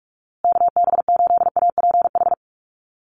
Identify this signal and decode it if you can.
Morse code, heard as a keyed tone, about 44 wpm, 700 Hz. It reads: KB8APH
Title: USA Call Signs Repeated 3x 45wpm